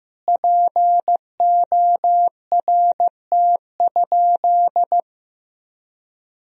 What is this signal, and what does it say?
Morse code, heard as a keyed tone, about 15 wpm, 700 Hz. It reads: PORT?
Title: Sentences from Top 1000 Words - Review Part 1 - 15wpm